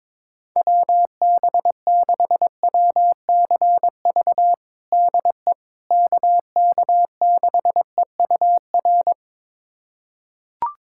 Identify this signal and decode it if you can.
Morse code, heard as a keyed tone, about 22 wpm, 700 Hz. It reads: WB6WCV DE KK6EUR
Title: USA Callsign DE Callsign - 6 Characters - 22wpm